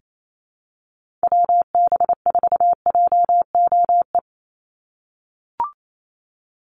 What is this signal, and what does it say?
Morse code, heard as a keyed tone, about 28 wpm, 700 Hz. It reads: WB4JOE
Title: USA Call Signs 28wpm